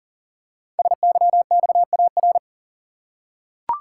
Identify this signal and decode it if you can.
Morse code, heard as a keyed tone, about 39 wpm, 700 Hz. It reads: SYXAR
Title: Code Groups - 5 Letters - 40wpm